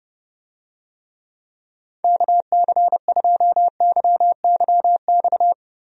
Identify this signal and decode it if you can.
Morse code, heard as a keyed tone, about 30 wpm, 700 Hz. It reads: KC2YYX
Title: USA Call Signs - 6 Characters - Repeated 2x - 30wpm